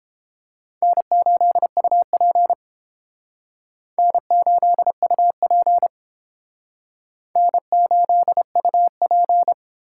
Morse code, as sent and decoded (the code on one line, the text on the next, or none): -. ---.. ..- .--. / -. ---.. ..- .--. / -. ---.. ..- .--.
N8UP N8UP N8UP